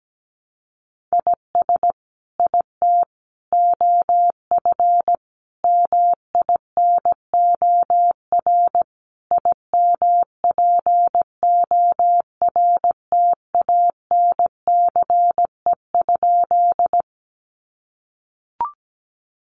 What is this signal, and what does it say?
Morse code, heard as a keyed tone, about 17 wpm, 700 Hz. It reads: IS IT OF MINOR IMPORTANCE?
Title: Sentences from Top 1500 Words 17wpm